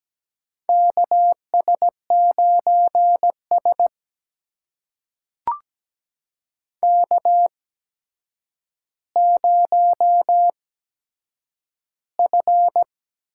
Morse code, heard as a keyed tone, about 17 wpm, spaced as for 8 wpm, 700 Hz. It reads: KS9S K0F
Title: USA Call Signs - 4 Characters - 8x Character Spacing - 17wpm